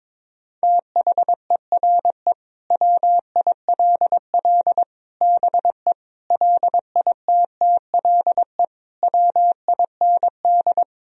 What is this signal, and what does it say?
Morse code, heard as a keyed tone, about 22 wpm, 700 Hz. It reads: THERE WILL BE LITTLE WIND
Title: Sentences from Top 1000 Words - Review Part 2 - 22wpm